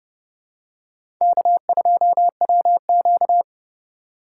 Morse code, heard as a keyed tone, about 30 wpm, 700 Hz. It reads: K2WQ
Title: USA Call Signs - 2x Character Spacing - 30wpm